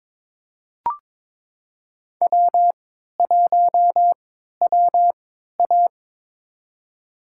Morse code, heard as a keyed tone, about 22 wpm, spaced as for 12 wpm, 700 Hz. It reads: W1WA
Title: USA Call Signs - 4 Characters - 3x Character Spacing - 22wpm